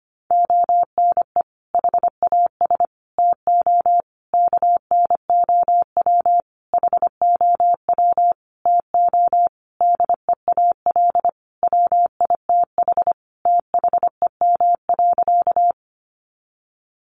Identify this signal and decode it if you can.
Morse code, heard as a keyed tone, about 25 wpm, 700 Hz. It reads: ONE HAS TO KNOW HOW TO DEAL WITH THEM.
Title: Sentences from Top 800 Words 25wpm